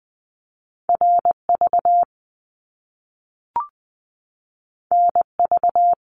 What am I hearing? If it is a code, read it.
Morse code, heard as a keyed tone, about 20 wpm, 700 Hz. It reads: RV NV
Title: Top Two Letter Combinations in Words 20wpm